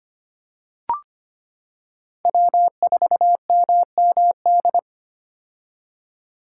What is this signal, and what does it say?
Morse code, heard as a keyed tone, about 25 wpm, 700 Hz. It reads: W4MMD
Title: K1USN SST QSO Practice - 25wpm